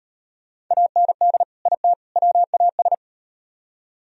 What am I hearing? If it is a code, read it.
Morse code, heard as a keyed tone, about 37 wpm, 700 Hz. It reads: AND IT WAS